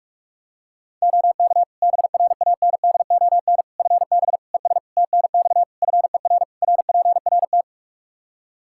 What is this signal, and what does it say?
Morse code, heard as a keyed tone, about 46 wpm, 700 Hz. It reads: OK BRANDON FB ES TNX FER RPRT
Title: K1USN SST QSO Practice 45wpm